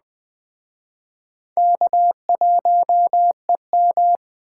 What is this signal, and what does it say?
Morse code, heard as a keyed tone, about 20 wpm, 700 Hz. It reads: K1EM